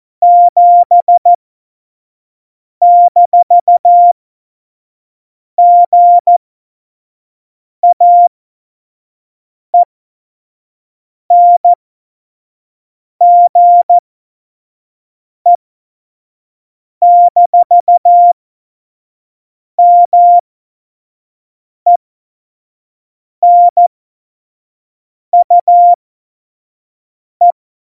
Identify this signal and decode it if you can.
Morse code, heard as a keyed tone, about 14 wpm, spaced as for 5 wpm, 700 Hz. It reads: 7-GAENGE-MENUE
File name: Morsekurs / Witze / Witz 11. Februar 2020, 25 F70